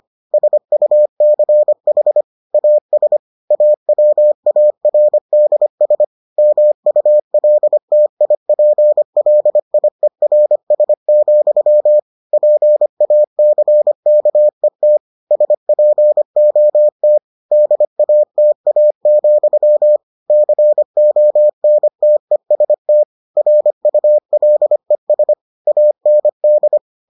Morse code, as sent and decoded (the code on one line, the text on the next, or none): ... ..- -.-. .... / .- ... / .- .-- .- .-. -.. ... / -- ..- .-.. - .. .--. .-.. .. .-. ... --..-- / .--. .- -.-. -.- . - / ... .--. --- - / -.. .- - .- --..-- / -.-. --- -. - . ... - / .-. ..- .-.. ... / .- -. -..
SUCH AS AWARDS MULTIPLIERS, PACKET SPOT DATA, CONTEST RULES AND